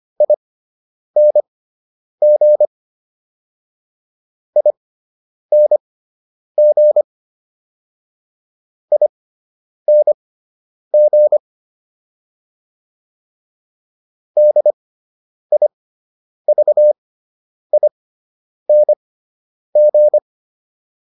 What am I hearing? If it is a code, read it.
Morse code, heard as a keyed tone, about 25 wpm, spaced as for 9 wpm, 600 Hz. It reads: ING ING ING DIVING